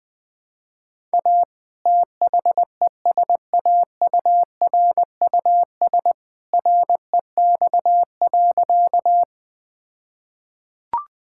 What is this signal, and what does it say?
Morse code, heard as a keyed tone, about 21 wpm, 700 Hz. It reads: A THESAURUS REX.
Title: All-Age Appropriate Jokes - 20wpm